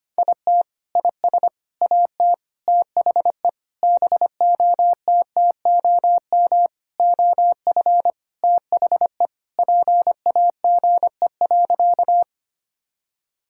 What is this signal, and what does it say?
Morse code, heard as a keyed tone, about 25 wpm, 700 Hz. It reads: IT IS AT THE BOTTOM OF THE PAGE.
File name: practice / 25wpm / Sentences with 4-Word Phrases - 25wpm